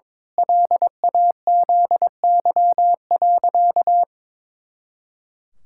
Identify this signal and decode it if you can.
Morse code, heard as a keyed tone, about 22 wpm, 700 Hz. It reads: LAZY.
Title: Pangram Challenge v2 - 22wpm